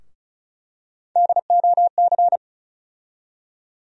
Morse code, heard as a keyed tone, about 36 wpm, 700 Hz. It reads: DOC
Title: USA Call Sign Postfixes - 3 Letters 35wpm